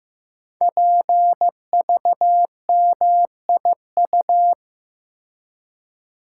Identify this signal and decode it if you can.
Morse code, heard as a keyed tone, about 15 wpm, 700 Hz. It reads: PVMIU
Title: Code Groups - 5 Letter:Numbers - 15wpm